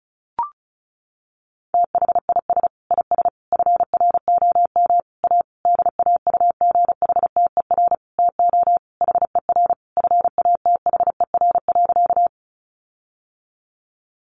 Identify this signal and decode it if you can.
Morse code, heard as a keyed tone, about 34 wpm, 700 Hz. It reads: THIS IS FROM A DAUGHTER TO HER FATHER.
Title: Sentences from Top 1500 Words 35wpm